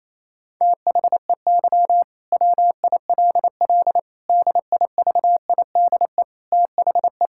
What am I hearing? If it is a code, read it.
Morse code, heard as a keyed tone, about 28 wpm, 700 Hz. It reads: THEY WILL DIVIDE THE
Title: Sentences from Top 1600 Words 28wpm